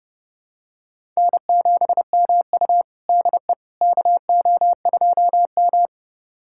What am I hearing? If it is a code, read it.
Morse code, heard as a keyed tone, about 30 wpm, 700 Hz. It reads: N7MU DE KO2M
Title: USA Callsign DE Callsign - 4 Characters - 30wpm